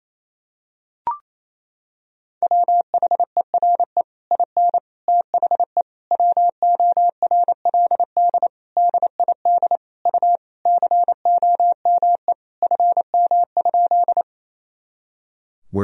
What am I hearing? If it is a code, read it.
Morse code, heard as a keyed tone, about 28 wpm, 700 Hz. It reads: WHERE IN THE WORLD DID U COME FM?
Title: Sentences from Top 200 Words and Abbreviations 28wpm